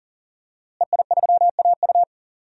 Morse code, heard as a keyed tone, about 40 wpm, 700 Hz. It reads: EI3AU
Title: DXCC-100 Call Signs - 5 Characters - Repeated 2x - 40wpm